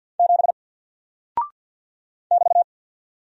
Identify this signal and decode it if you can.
Morse code, heard as a keyed tone, about 48 wpm, 700 Hz. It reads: 7 <BT>
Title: Single Character (Letter:Number:Punctuation:Prosign) Alt 50wpm